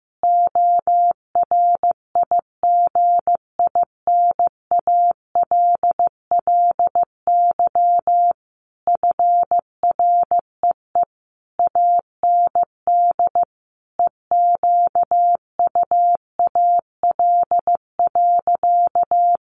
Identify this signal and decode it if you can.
Morse code, heard as a keyed tone, about 15 wpm, 700 Hz. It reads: ORIGINALLY FREE AND EQUAL.
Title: Sentences from Top 1500 Words 15wpm